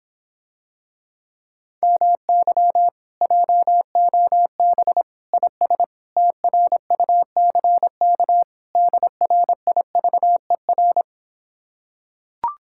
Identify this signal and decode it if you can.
Morse code, heard as a keyed tone, about 26 wpm, 700 Hz. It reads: MY JOB IS TRUCK DRIVER